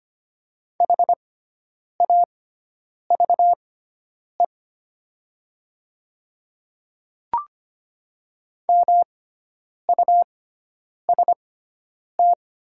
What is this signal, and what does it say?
Morse code, heard as a keyed tone, about 25 wpm, spaced as for 9 wpm, 700 Hz. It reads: HAVE MUST
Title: 6x Character Spacing - Top 500 Words - 25wpm